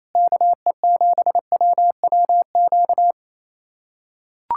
Morse code, heard as a keyed tone, about 28 wpm, 700 Hz. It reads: KE7WWQ
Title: USA Callsign DE Callsign 28wpm